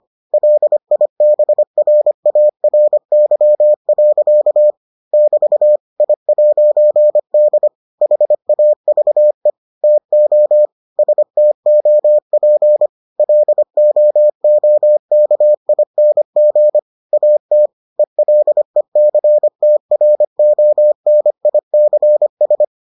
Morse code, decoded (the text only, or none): LIBRARY. <BT> I'D HAVE TO STOP LOOKING AT ELECTRONICS